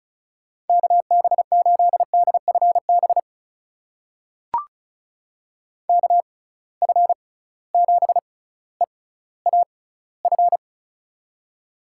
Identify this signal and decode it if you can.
Morse code, heard as a keyed tone, about 34 wpm, spaced as for 18 wpm, 700 Hz. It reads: KB8DFB KF7EAF
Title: USA Call Signs - 6 Characters - 6x Character Spacing - 35wpm